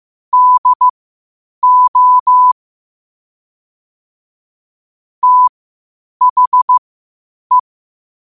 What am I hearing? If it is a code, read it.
Morse code, heard as a keyed tone, about 15 wpm, spaced as for 8 wpm, 1000 Hz. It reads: DO THE
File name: qotdfiles / 8wpm